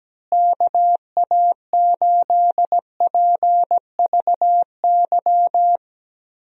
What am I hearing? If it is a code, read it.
Morse code, heard as a keyed tone, about 17 wpm, 700 Hz. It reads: KA8PVY
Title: USA Call Signs - 6 Characters - Repeated 3x - 17wpm